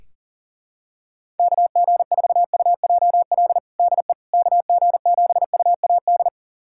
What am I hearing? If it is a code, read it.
Morse code, heard as a keyed tone, about 40 wpm, 700 Hz. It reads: KG4UJL DE KG7UAD
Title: USA Callsign DE Callsign 40wpm